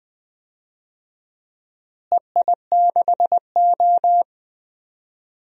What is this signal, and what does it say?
Morse code, heard as a keyed tone, about 20 wpm, 700 Hz. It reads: EI6O
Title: DXCC-100 Call Signs - 4 Characters - Repeated 2x - 20wpm